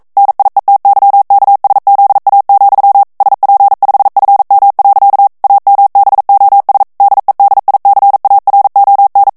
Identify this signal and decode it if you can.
Morse code, heard as a keyed tone, about 42 wpm, 800 Hz. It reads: NIETYKSZA, SP5FM. AMBOS DEDICARON
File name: morse mp3 / ESPAGNOL TEXTE EN CLAIR 1 VITESSE 40 MOTS MINUTES